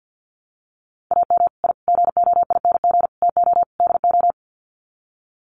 Morse code, heard as a keyed tone, about 50 wpm, 700 Hz. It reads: AM I GOING TO DO
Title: Sets of 5 Words Encore 50wpm